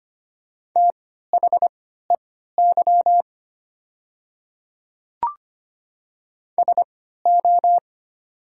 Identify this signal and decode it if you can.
Morse code, heard as a keyed tone, about 25 wpm, spaced as for 14 wpm, 700 Hz. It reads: THEY SO
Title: 3x Character Spacing - Top 100 Words - 25wpm